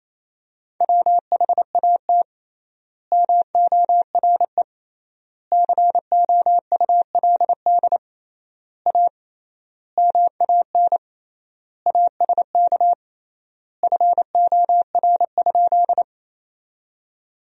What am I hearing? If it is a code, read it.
Morse code, heard as a keyed tone, about 27 wpm, 700 Hz. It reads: WHAT MORE COULD A MAN ASK FOR?
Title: 3x Word-Spacing - Sentences from Top 200 Words - 28wpm